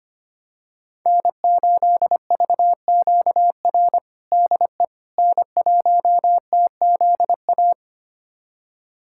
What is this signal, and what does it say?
Morse code, heard as a keyed tone, about 25 wpm, 700 Hz. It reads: N8VQR DE N1TZA
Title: USA Callsign DE Callsign - 5 Characters - 25wpm